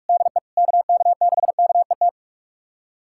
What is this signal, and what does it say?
Morse code, heard as a keyed tone, about 41 wpm, 700 Hz. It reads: DE KK6KET
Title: USA Callsign DE Callsign - 6 Characters - 45wpm